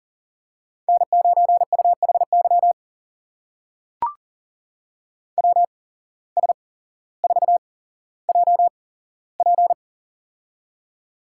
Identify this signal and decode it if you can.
Morse code, heard as a keyed tone, about 40 wpm, spaced as for 17 wpm, 700 Hz. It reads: N9UHY WS4JP
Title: USA Call Signs - 5 Characters - 8x Character Spacing - 40wpm